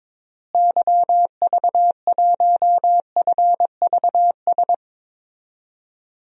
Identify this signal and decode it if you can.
Morse code, heard as a keyed tone, about 21 wpm, 700 Hz. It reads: YV1FVS
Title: DXCC-100 Call Signs - 6 Characters - Repeated 2x - 22wpm